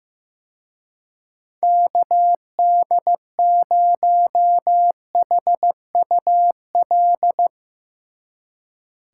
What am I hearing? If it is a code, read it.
Morse code, heard as a keyed tone, about 16 wpm, 700 Hz. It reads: KD0HUL
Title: Speed Racing - USA 2x3 Call Signs 15wpm